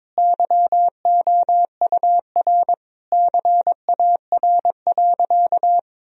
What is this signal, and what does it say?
Morse code, heard as a keyed tone, about 22 wpm, 700 Hz. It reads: YOUR CAR.
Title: Sentences from Top 1000 Words - Review Part 1 - 22wpm